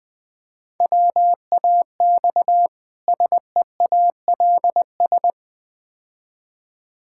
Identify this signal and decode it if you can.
Morse code, heard as a keyed tone, about 20 wpm, 700 Hz. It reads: WAX SEALS
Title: Difficult Words with Hints - 20wpm